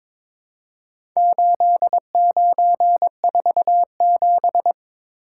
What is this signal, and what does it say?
Morse code, heard as a keyed tone, about 23 wpm, 700 Hz. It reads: 8947
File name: practice / 22wpm / Four Letter-Number Combinations - 22wpm